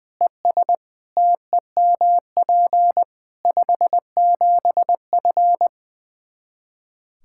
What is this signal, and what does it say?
Morse code, heard as a keyed tone, about 20 wpm, 700 Hz. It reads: ES TEMP 57F
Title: Practice QSO - Exchange 2 - Info - 20wpm